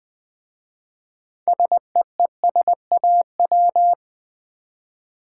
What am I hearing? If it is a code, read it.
Morse code, heard as a keyed tone, about 20 wpm, 700 Hz. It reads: SEESAW
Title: Easy Words with Hints - 20wpm